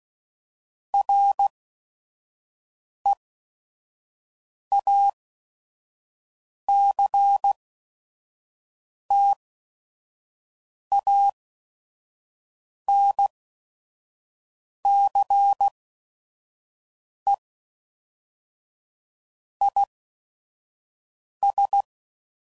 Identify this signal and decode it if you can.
Morse code, heard as a keyed tone, about 16 wpm, spaced as for 5 wpm, 800 Hz. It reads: REACTANCE IS
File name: CW Code Groups @5WPM / Technical Subjects / XC